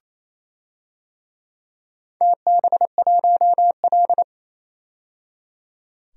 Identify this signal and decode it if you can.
Morse code, heard as a keyed tone, about 27 wpm, 700 Hz. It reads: TB1L
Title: DXCC-100 Call Signs - 4 Characters - 28wpm